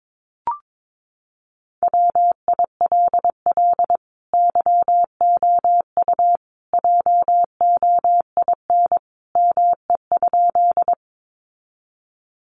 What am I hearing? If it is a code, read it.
Morse code, heard as a keyed tone, about 22 wpm, 700 Hz. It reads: WILL YOU JOIN ME?
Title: Sentences from Top 1400 Words 22wpm